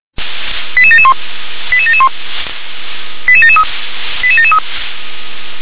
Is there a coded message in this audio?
Yes: five-tone selective call (ZVEI) from 0.8 to 4.6 s, heard as quick runs of five tones, tones of about 70 ms. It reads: 88771 88771 88773 88773